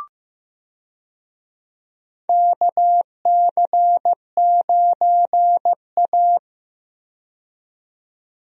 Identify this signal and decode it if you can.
Morse code, heard as a keyed tone, about 15 wpm, 700 Hz. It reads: KC9A
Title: USA Call Signs - 4 Characters - 15wpm